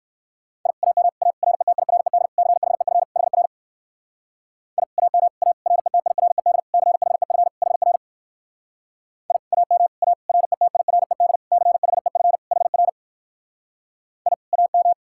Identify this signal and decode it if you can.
Morse code, heard as a keyed tone, about 62 wpm, 700 Hz. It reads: I AM A RETIRED CHEF HR I AM A RETIRED CHEF HR I AM A RETIRED CHEF HR I AM